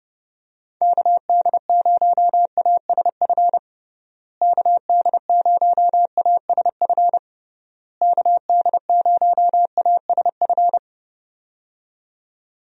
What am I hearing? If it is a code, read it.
Morse code, heard as a keyed tone, about 32 wpm, 700 Hz. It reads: KD0ASF KD0ASF KD0ASF
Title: USA Call Signs - 6 Characters - Repeated 3x - 30wpm